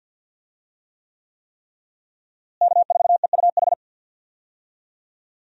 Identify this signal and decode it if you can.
Morse code, heard as a keyed tone, about 50 wpm, 700 Hz. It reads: K4EUH
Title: USA Call Signs - 5 Characters - Repeated 3x - 50wpm